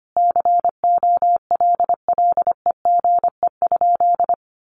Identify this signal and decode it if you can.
Morse code, heard as a keyed tone, about 25 wpm, 700 Hz. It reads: COLLEGE?
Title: Sentences from Top 1300 Words 25wpm